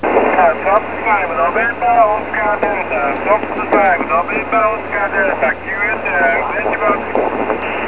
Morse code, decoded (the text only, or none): none